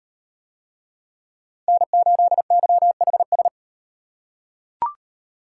Morse code, heard as a keyed tone, about 36 wpm, 700 Hz. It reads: N8YHS